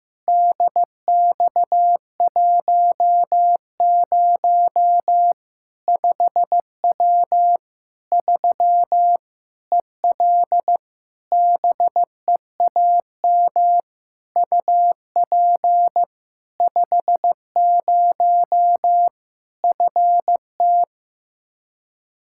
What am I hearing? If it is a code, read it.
Morse code, heard as a keyed tone, about 15 wpm, 700 Hz. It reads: DX10 5W 3 EL BEAM UP 50 FT
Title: Practice QSO - Exchange 2 - Info - 15wpm